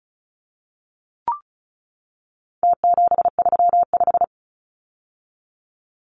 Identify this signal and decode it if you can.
Morse code, heard as a keyed tone, about 34 wpm, 700 Hz. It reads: T735
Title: Numbers - Contest Style - 4-Digit 35wpm